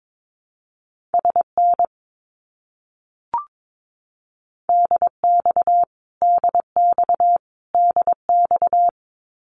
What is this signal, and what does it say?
Morse code, heard as a keyed tone, about 22 wpm, 700 Hz. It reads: SN DX DX DX
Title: Abbreviations Repeated 3x - 22wpm